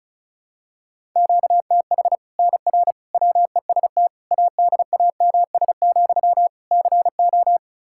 Morse code, heard as a keyed tone, about 36 wpm, 700 Hz. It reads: QTH NR WEST ADAMS, CO
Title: Practice QSO - Exchange 1 - Beginning - 35wpm